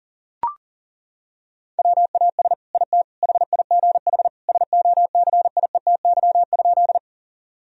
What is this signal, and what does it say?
Morse code, heard as a keyed tone, about 40 wpm, 700 Hz. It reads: WAS IT HIGH SOCIETY?
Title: Sentences from Top 1000 Words - Review Part 4 - 40wpm